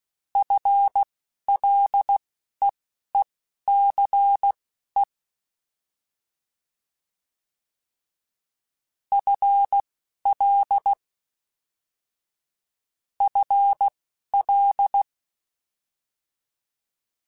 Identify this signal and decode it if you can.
Morse code, heard as a keyed tone, about 16 wpm, spaced as for 11 wpm, 800 Hz. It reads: FLEECE FL FL